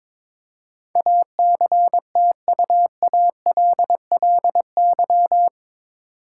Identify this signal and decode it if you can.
Morse code, heard as a keyed tone, about 22 wpm, 700 Hz. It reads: ACTUALLY
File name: practice / 22wpm / Eight Letter Words - 22wpm